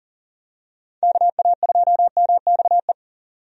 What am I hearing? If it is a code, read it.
Morse code, heard as a keyed tone, about 40 wpm, 700 Hz. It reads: KA2MXE
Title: USA Call Signs - 6 Characters - 40wpm